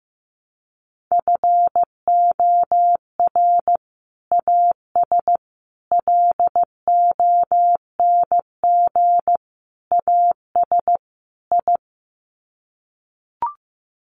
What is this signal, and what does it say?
Morse code, heard as a keyed tone, about 15 wpm, 700 Hz. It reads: FOR AS LONG AS I
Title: Sets of 5 Words Encore2 15wpm